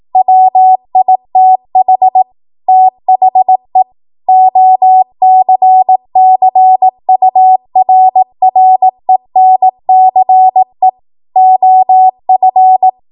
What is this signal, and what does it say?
Morse code, heard as a keyed tone, about 18 wpm, 750 Hz. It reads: WITH THE OCCURRENCE OF
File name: Morse / 18WPM